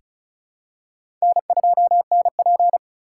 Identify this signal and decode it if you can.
Morse code, heard as a keyed tone, about 36 wpm, 700 Hz. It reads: N2NP
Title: USA Call Signs - 4 Characters - 2x Character Spacing - 35wpm